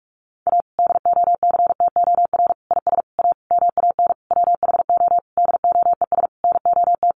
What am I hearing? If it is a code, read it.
Morse code, heard as a keyed tone, about 44 wpm, 700 Hz. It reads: A DOCTOR IS A MAN WHO DOES NOT
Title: Sentences from Top 1500 Words 45wpm